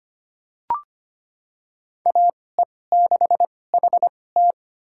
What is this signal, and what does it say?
Morse code, heard as a keyed tone, about 25 wpm, spaced as for 18 wpm, 700 Hz. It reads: AE6HT
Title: USA Call Signs - 2x Character Spacing - 25wpm